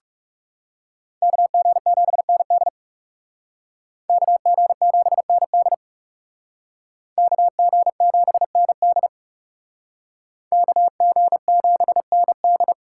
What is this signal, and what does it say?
Morse code, heard as a keyed tone, about 37 wpm, 700 Hz. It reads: KG7ND KG7ND KG7ND KG7ND
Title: Speed Racing - USA Call Signs 30wpm